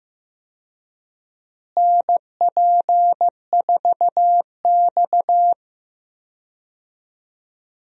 Morse code, heard as a keyed tone, about 15 wpm, 700 Hz. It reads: NP4X
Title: USA Call Signs - 4 Characters - Repeated 3x - 15wpm